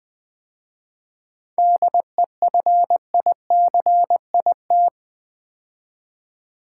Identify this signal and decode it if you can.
Morse code, heard as a keyed tone, about 20 wpm, 700 Hz. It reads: DEFICIT
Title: 7 Letter Words - First 3 Letters Given - 20wpm